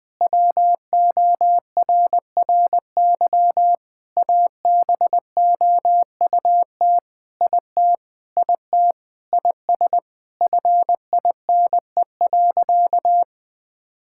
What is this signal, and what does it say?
Morse code, heard as a keyed tone, about 20 wpm, 700 Hz. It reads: WORRY ABOUT IT IT IS FINE.
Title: Sentences with 5-Word Phrases - 20wpm